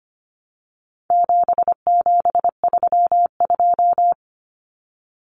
Morse code, heard as a keyed tone, about 25 wpm, 700 Hz. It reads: 7732
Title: Common Cut Numbers - 4-Digit 25wpm